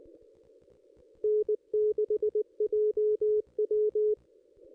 Morse code, heard as a keyed tone, about 23 wpm, 420 Hz. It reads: N6JW
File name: contests / ss-cw-k5tr-op-n5rz -1400